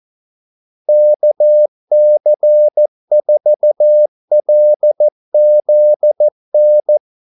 Morse code, heard as a keyed tone, about 14 wpm, 600 Hz. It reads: KC4LZN
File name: Callsigns-14wpm